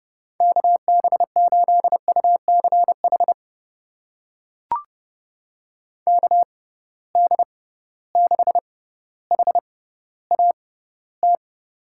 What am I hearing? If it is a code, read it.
Morse code, heard as a keyed tone, about 32 wpm, spaced as for 16 wpm, 700 Hz. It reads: KB8UCH KD6HAT
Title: USA Call Signs - 6 Characters - 6x Character Spacing - 30wpm